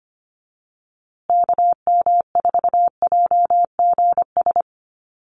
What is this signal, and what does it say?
Morse code, heard as a keyed tone, about 25 wpm, 700 Hz. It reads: KM4JGS
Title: USA Call Signs Repeated 3x 25wpm